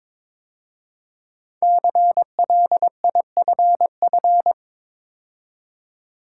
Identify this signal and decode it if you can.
Morse code, heard as a keyed tone, about 21 wpm, 700 Hz. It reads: CLIFF
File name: 5 Letter Words - First 3 Letters Given - 22wpm